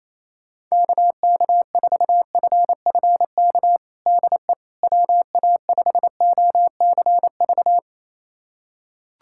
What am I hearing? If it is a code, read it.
Morse code, heard as a keyed tone, about 28 wpm, 700 Hz. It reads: KK4FFK DE WA5OCV
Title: USA Callsign DE Callsign - 6 Characters - 28wpm